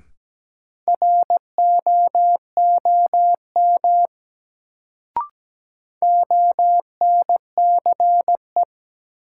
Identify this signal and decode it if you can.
Morse code, heard as a keyed tone, about 17 wpm, 700 Hz. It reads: ROOM ONCE
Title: Sentences from Top 300 Words - New Word Drill - 17wpm